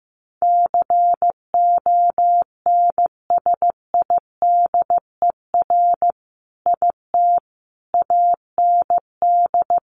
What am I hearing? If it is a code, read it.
Morse code, heard as a keyed tone, about 14 wpm, 700 Hz. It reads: CONSIDER IT AND
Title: Sentences from Top 1400 Words 15wpm